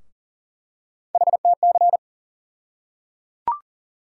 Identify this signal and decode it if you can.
Morse code, heard as a keyed tone, about 38 wpm, 700 Hz. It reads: HTC
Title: USA Call Sign Postfixes - 3 Letters 40wpm